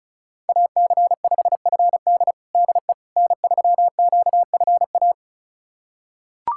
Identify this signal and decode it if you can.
Morse code, heard as a keyed tone, about 34 wpm, 700 Hz. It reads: AC5FD DE N3QFA
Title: USA Callsign DE Callsign - 5 Characters - 35wpm